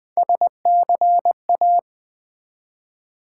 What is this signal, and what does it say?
Morse code, heard as a keyed tone, about 20 wpm, 700 Hz. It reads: SCA
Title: Build A Word - 1 Letter At A Time - 4-Letter Words - 20wpm